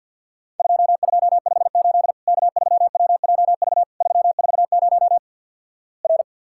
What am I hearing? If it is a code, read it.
Morse code, heard as a keyed tone, about 53 wpm, 700 Hz. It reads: 2258 K3WJV 340 R
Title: Worked All Europe DX Contest - QTC Exchange - 50wpm